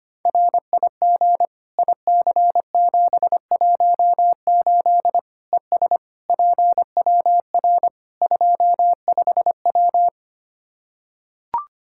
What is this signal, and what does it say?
Morse code, heard as a keyed tone, about 25 wpm, 700 Hz. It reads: RIG IC718 ES PWR 25W
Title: Practice QSO - Exchange 2 - Info - 25wpm